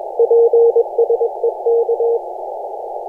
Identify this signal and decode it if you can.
Morse code, heard as a keyed tone, about 21 wpm, 470 Hz. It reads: PSE K
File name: posts / 2016-jan-16-2256z-e51j-18-090mhz-cw-cqing